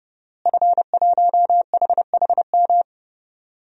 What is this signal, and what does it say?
Morse code, heard as a keyed tone, about 30 wpm, 700 Hz. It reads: F1HHM
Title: DXCC-100 Call Signs - 5 Characters - Repeated 2x - 30wpm